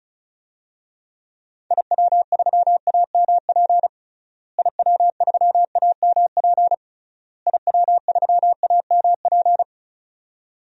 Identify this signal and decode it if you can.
Morse code, heard as a keyed tone, about 34 wpm, 700 Hz. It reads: IW3AMP IW3AMP IW3AMP
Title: DXCC-100 Call Signs - 6 Characters - Repeated 3x - 35wpm